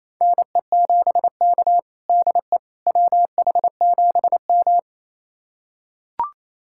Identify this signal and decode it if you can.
Morse code, heard as a keyed tone, about 28 wpm, 700 Hz. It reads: NE7K DE WH7M
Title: USA Callsign DE Callsign - 4 Characters - 28wpm